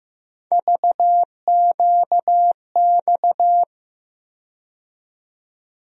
Morse code, heard as a keyed tone, about 15 wpm, 700 Hz. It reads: VQX
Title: USA Call Sign Postfixes - 3 Letters 15wpm